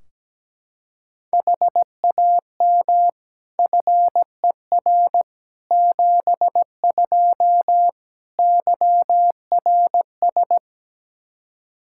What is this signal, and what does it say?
Morse code, heard as a keyed tone, about 17 wpm, 700 Hz. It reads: HAM FER 72 YRS
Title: K1USN SST QSO Practice 17wpm